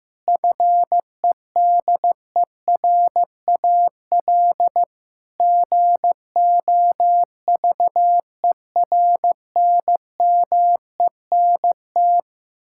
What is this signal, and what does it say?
Morse code, heard as a keyed tone, about 15 wpm, 700 Hz. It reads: FEDERAL GOVERNMENT
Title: Sentences from Top 1000 Words - Review Part 5 - 15wpm